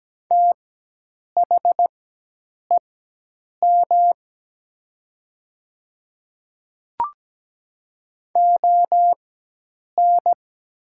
Morse code, heard as a keyed tone, about 17 wpm, spaced as for 8 wpm, 700 Hz. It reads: THEM ON
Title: 4x Character Spacing - Top 100 Words - 17wpm